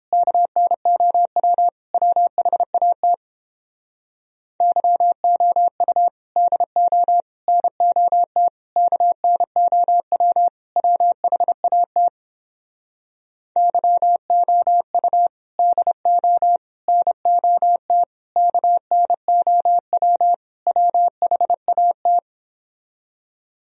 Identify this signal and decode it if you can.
Morse code, heard as a keyed tone, about 29 wpm, 700 Hz. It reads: KNOW WHAT YOU DO NOT KNOW WHAT YOU DO NOT KNOW WHAT